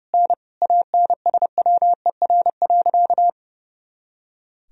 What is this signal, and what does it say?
Morse code, heard as a keyed tone, about 30 wpm, 700 Hz. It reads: N ANSWER.
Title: Sentences from Top 500 Words and Abbreviations 30wpm